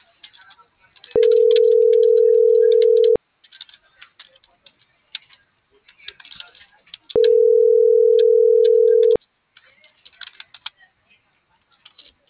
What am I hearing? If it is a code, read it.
Ringback tone (North America), 440 + 480 Hz.